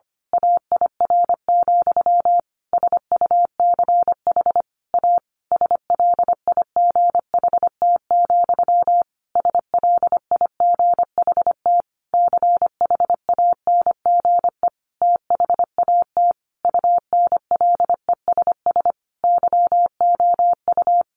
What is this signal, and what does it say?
Morse code, heard as a keyed tone, about 25 wpm, 700 Hz. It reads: AIR, SUCH A SLIGHT, SLIGHT CHANGE THAT UNLESS YOU
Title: A Sound Of Thunder 25wpm